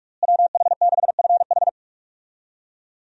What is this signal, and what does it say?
Morse code, heard as a keyed tone, about 46 wpm, 700 Hz. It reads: WH6FH
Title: USA Call Signs - 5 Characters - Repeated 3x - 45wpm